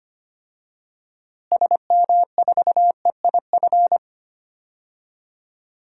Morse code, heard as a keyed tone, about 25 wpm, 700 Hz. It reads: SM4EIF
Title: DXCC-100 Call Signs - 6 Characters - Repeated 3x - 25wpm